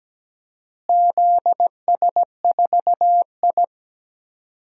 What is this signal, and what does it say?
Morse code, heard as a keyed tone, about 17 wpm, 700 Hz. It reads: ZS4I
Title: DXCC-100 Call Signs - 4 Characters - Repeated 2x - 17wpm